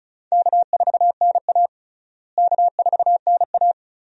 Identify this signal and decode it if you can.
Morse code, heard as a keyed tone, about 34 wpm, 700 Hz. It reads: K4NA K4NA
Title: USA Call Signs - 4 Characters - Repeated 2x - 35wpm